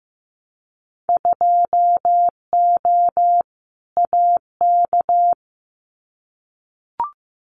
Morse code, heard as a keyed tone, about 15 wpm, 700 Hz. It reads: 2O AK
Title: Winter Field Day Exchanges Encore 15wpm